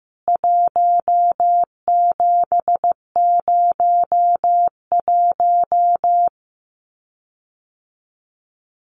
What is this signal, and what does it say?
Morse code, heard as a keyed tone, about 15 wpm, 700 Hz. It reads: 1701
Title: CWT Exchange 15wpm